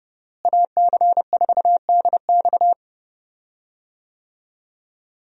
Morse code, heard as a keyed tone, about 30 wpm, 700 Hz. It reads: AC4DX
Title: USA Call Signs - 5 Characters - Repeated 3x - 30wpm